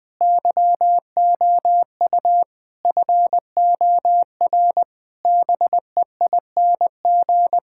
Morse code, heard as a keyed tone, about 20 wpm, 700 Hz. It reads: YOU FOR BEING
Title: Sentences with 7-Word Phrases - 20wpm